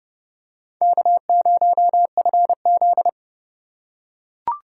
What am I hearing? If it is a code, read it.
Morse code, heard as a keyed tone, about 30 wpm, 690 Hz. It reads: K0FZ
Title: USA Call Signs - 4x Character Spacing - 30wpm